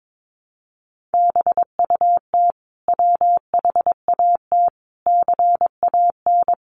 Morse code, heard as a keyed tone, about 22 wpm, 700 Hz. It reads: BUT WHAT CAN
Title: Sentences from Top 1100 Words 22wpm